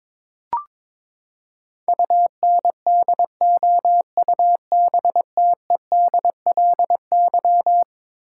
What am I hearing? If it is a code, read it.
Morse code, heard as a keyed tone, about 22 wpm, 700 Hz. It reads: UNDOUBTEDLY
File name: practice / 22wpm / Eleven Letter Words - 22wpm